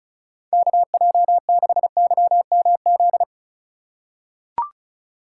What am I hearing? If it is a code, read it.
Morse code, heard as a keyed tone, about 34 wpm, 700 Hz. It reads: KJ6YMZ
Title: USA Call Signs - 6 Characters - 2x Character Spacing - 35wpm